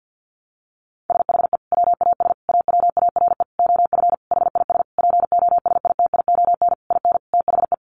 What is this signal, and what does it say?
Morse code, heard as a keyed tone, about 50 wpm, 700 Hz. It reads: SHE WAS AWARE OF HIS POSITION IN THE
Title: Sentences from Top 1200 Words 50wpm